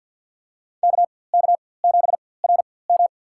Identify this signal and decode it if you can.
Morse code, heard as a keyed tone, about 53 wpm, spaced as for 24 wpm, 700 Hz. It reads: KK7RM
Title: USA Call Signs - 5 Characters - 4x Character Spacing - 50wpm